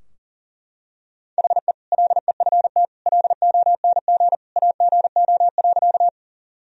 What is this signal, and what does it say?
Morse code, heard as a keyed tone, about 40 wpm, 700 Hz. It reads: HE LEFT LONG AGO.
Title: Sentences from Top 1000 Words - Review Part 3 - 40wpm